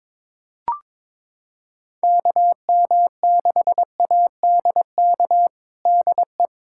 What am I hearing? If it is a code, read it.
Morse code, heard as a keyed tone, about 22 wpm, 700 Hz. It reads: KM6ADK DE
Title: USA Callsign DE Callsign - 6 Characters - 22wpm